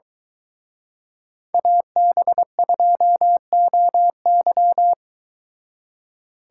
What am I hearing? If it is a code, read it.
Morse code, heard as a keyed tone, about 22 wpm, 700 Hz. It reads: AB2OY